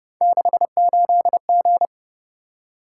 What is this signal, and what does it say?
Morse code, heard as a keyed tone, about 30 wpm, 700 Hz. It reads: 68G